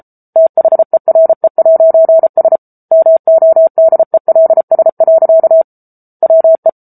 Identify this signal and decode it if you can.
Morse code, heard as a keyed tone, about 34 wpm, 650 Hz. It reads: THERE'S MODELS. WE
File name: Book Two / 07-Chapter 7 - 33 WPM